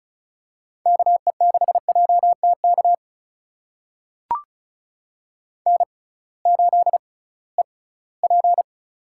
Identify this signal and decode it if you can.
Morse code, heard as a keyed tone, about 36 wpm, spaced as for 21 wpm, 700 Hz. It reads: KE6JTK N8EP
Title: USA Call Signs - 6x Character Spacing - 35wpm